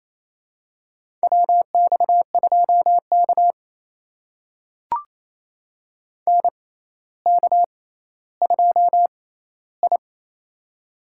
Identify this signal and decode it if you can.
Morse code, heard as a keyed tone, about 28 wpm, spaced as for 14 wpm, 700 Hz. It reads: WX2K NK2I
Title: USA Call Signs - 4 Characters - 6x Character Spacing - 28wpm